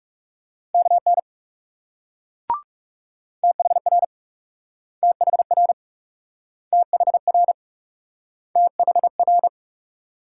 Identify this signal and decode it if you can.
Morse code, heard as a keyed tone, about 38 wpm, 700 Hz. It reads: KN THR THR THR THR